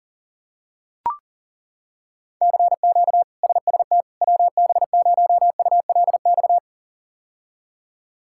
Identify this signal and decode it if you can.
Morse code, heard as a keyed tone, about 40 wpm, 700 Hz. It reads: CQ SST WB0ULX
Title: K1USN SST QSO Practice - 40wpm